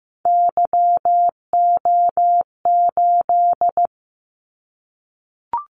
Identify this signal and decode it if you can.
Morse code, heard as a keyed tone, about 15 wpm, 700 Hz. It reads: YO8
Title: Callsign Prefixes for Top 25 DXCC - 15wpm